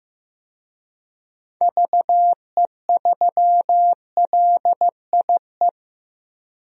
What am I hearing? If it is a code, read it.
Morse code, heard as a keyed tone, about 16 wpm, 700 Hz. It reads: VE3LIE
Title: DXCC-100 Call Signs - 6 Characters - 15wpm